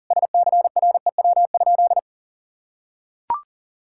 Morse code, heard as a keyed tone, about 40 wpm, 700 Hz. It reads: SCREW?
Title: Sentences from Top 1600 Words 40wpm